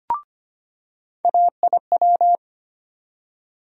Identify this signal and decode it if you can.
Morse code, heard as a keyed tone, about 25 wpm, 700 Hz. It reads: AIW
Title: USA Call Sign Postfixes - 3 Letters 25wpm